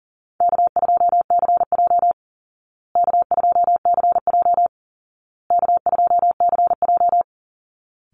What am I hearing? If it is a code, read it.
Morse code, heard as a keyed tone, about 40 wpm, 700 Hz. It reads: K2CJ K2CJ K2CJ
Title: Call Signs Repeated 2x - 40wpm